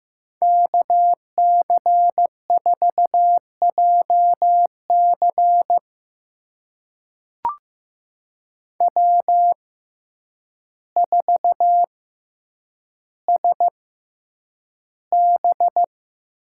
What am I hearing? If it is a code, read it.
Morse code, heard as a keyed tone, about 15 wpm, spaced as for 8 wpm, 700 Hz. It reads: KC4JC W4SB
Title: USA Call Signs - 5 Characters - 6x Character Spacing - 15wpm